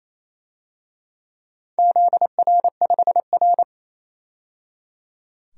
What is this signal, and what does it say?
Morse code, heard as a keyed tone, about 28 wpm, 700 Hz. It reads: ZR5R
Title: DXCC-100 Call Signs - 4 Characters - 28wpm